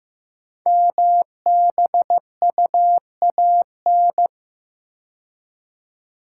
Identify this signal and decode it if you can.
Morse code, heard as a keyed tone, about 15 wpm, 700 Hz. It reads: MBUAN
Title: Code Groups - 5 Letters - 15wpm